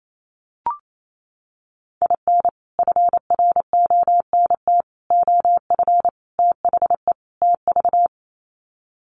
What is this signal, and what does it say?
Morse code, heard as a keyed tone, about 28 wpm, 700 Hz. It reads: IN FRONT OF THE TV
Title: Sets of 5 Words Encore2 28wpm